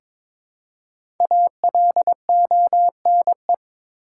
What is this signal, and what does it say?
Morse code, heard as a keyed tone, about 22 wpm, 700 Hz. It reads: ALONE
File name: 5 Letter Words - First 3 Letters Given - 22wpm